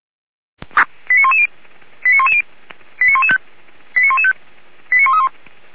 Five-tone selective call (ZVEI), heard as quick runs of five tones, tones of about 70 ms. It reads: 89110 89110 89115 89115 89131